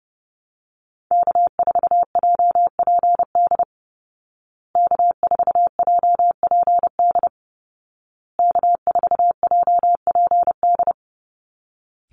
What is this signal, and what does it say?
Morse code, heard as a keyed tone, about 30 wpm, 700 Hz. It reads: K4JPD K4JPD K4JPD
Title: Call Signs Repeated 2x - 30wpm